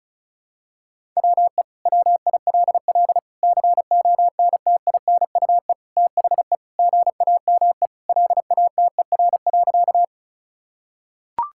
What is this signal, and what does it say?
Morse code, heard as a keyed tone, about 36 wpm, 700 Hz. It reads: WE WILL CONTINUE THE GAME LATER.
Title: Sentences from Top 1000 Words - Review Part 6 - 35wpm